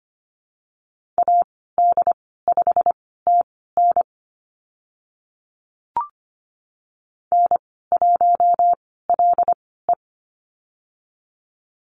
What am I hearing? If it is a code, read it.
Morse code, heard as a keyed tone, about 25 wpm, spaced as for 16 wpm, 700 Hz. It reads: AD5TN N1LE
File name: Lesson 15g - Number 5 - Call Signs 25wpm